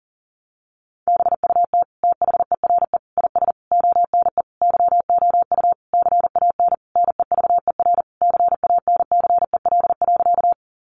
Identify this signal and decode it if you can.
Morse code, heard as a keyed tone, about 40 wpm, 700 Hz. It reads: BUT THERE IS ONE YOU CAN NEVER CANCEL.
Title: Sentences from Top 1200 Words 40wpm